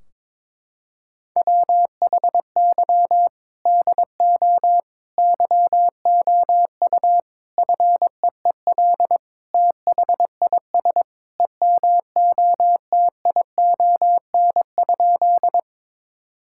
Morse code, heard as a keyed tone, about 21 wpm, 700 Hz. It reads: WHY DO YOU FEEL THIS EMOTION?
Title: Sentences from Top 1600 Words 22wpm